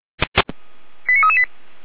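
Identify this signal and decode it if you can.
Five-tone selective call (ZVEI), heard as quick runs of five tones, tones of about 70 ms. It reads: 89338
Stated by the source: ZVEI five-tone selective call received on an SDR.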